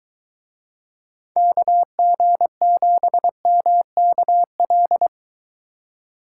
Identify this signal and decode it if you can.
Morse code, heard as a keyed tone, about 23 wpm, 700 Hz. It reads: KG7MKL